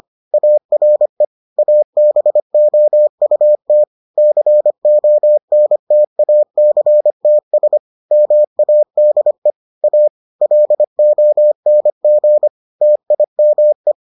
Morse code, heard as a keyed tone, about 25 wpm, 600 Hz. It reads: ARE ABOUT CONTACTS MADE A LONG TIME